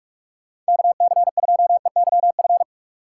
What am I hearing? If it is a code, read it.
Morse code, heard as a keyed tone, about 46 wpm, 700 Hz. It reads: KC2EYF
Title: USA Call Signs - 6 Characters - 3x Character Spacing - 45wpm